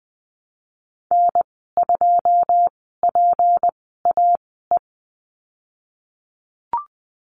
Morse code, heard as a keyed tone, about 20 wpm, spaced as for 14 wpm, 700 Hz. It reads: N2PAE
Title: 179 Lesson 29g - Slash - Call Signs 20wpm